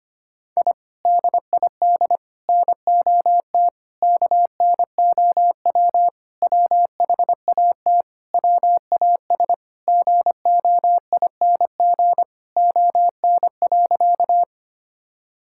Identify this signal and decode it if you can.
Morse code, heard as a keyed tone, about 25 wpm, 700 Hz. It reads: I DID NOT KNOW WHAT WAS GOING ON.
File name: practice / 25wpm / Sentences with 4-Word Phrases - 25wpm